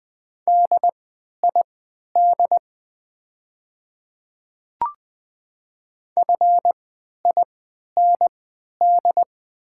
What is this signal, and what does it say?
Morse code, heard as a keyed tone, about 20 wpm, spaced as for 11 wpm, 700 Hz. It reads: DID FIND
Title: 3x Character Spacing - Top 100 Words - 20wpm